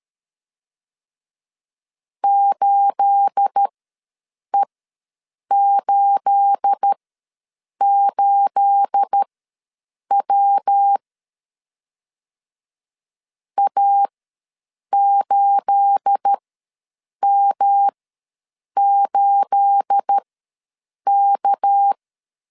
Morse code, heard as a keyed tone, about 13 wpm, spaced as for 7 wpm, 800 Hz. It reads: 8E88W A8M8K
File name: LEZIONI / 9 28 ws 12wpm 3354